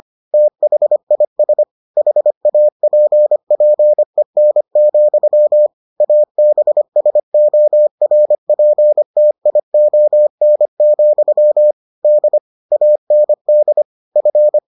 Morse code, heard as a keyed tone, about 25 wpm, 600 Hz. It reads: THIS HAPPEN, ABSORPTION, D AND F